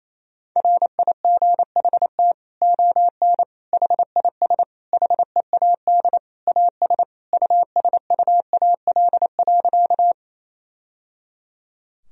Morse code, heard as a keyed tone, about 28 wpm, 700 Hz. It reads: RIGHT ON HIS HEAD AS USUAL.
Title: Sentences from Top 1000 Words - Review Part 2 - 28wpm